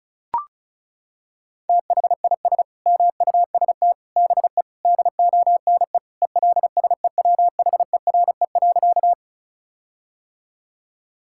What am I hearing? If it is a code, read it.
Morse code, heard as a keyed tone, about 36 wpm, 700 Hz. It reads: THIS MUST BE DONE ELSEWHERE.
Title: Sentences from Top 1600 Words 35wpm